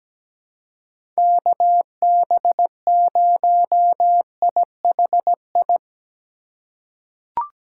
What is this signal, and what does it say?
Morse code, heard as a keyed tone, about 17 wpm, 700 Hz. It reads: KB0IHI
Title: USA Call Signs - 6 Characters - 2x Character Spacing - 17wpm